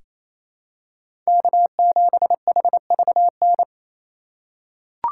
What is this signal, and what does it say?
Morse code, heard as a keyed tone, about 28 wpm, 700 Hz. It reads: K7HVN
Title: USA Call Signs - 5 Characters - 2x Character Spacing - 28wpm